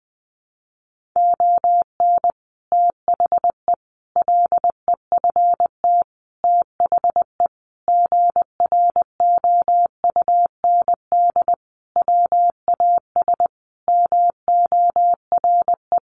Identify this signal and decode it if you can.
Morse code, heard as a keyed tone, about 20 wpm, 700 Hz. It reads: ON THE LEFT THE GROUND WAS MORE
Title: Sentences from Top 700 Words 20wpm